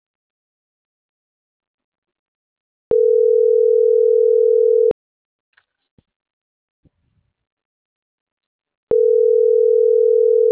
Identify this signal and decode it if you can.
Ringback tone (North America), 440 + 480 Hz.